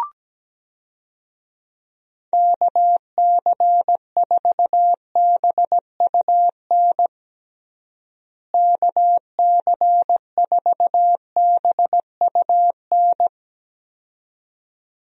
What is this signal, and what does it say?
Morse code, heard as a keyed tone, about 17 wpm, 700 Hz. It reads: KC4BUN KC4BUN
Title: USA Call Signs - 6 Characters - Repeated 2x - 17wpm